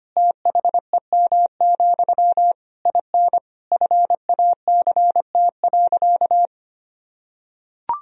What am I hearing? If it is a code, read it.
Morse code, heard as a keyed tone, about 25 wpm, 700 Hz. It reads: THEM, IN FACT.
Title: Sentences from Top 1000 Words - Review Part 3 - 25wpm